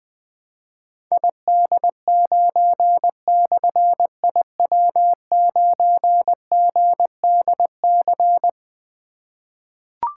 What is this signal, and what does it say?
Morse code, heard as a keyed tone, about 20 wpm, 700 Hz. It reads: ID9/IW9GDC
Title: Difficult Call Signs 20wpm